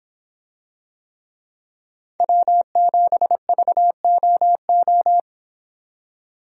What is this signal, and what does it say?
Morse code, heard as a keyed tone, about 25 wpm, 700 Hz. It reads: W7VOO